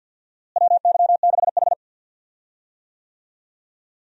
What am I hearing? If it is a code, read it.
Morse code, heard as a keyed tone, about 52 wpm, 700 Hz. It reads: WY6H
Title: USA Call Signs - 4 Characters - Repeated 3x - 50wpm